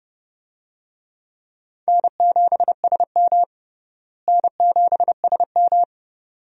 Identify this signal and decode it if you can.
Morse code, heard as a keyed tone, about 29 wpm, 700 Hz. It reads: N7SM N7SM
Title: USA Call Signs - 4 Characters - Repeated 2x - 30wpm